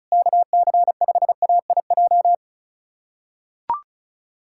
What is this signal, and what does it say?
Morse code, heard as a keyed tone, about 36 wpm, 700 Hz. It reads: KC5AIJ
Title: USA Callsign DE Callsign - 6 Characters - 35wpm